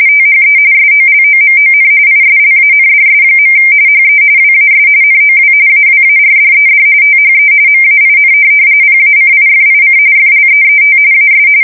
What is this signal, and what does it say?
RTTY, about 45.45 baud. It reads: THE SOLOMON ISLANDS. HE WAS NOW AWARDED THE MAIN PACIFIC FLEET AS
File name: fsk170 / USS-New-Jersey